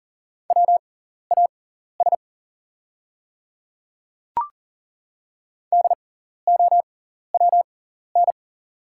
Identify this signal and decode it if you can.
Morse code, heard as a keyed tone, about 40 wpm, spaced as for 14 wpm, 690 Hz. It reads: WAS DOWN
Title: 6x Character Spacing - Top 100 Words - 40wpm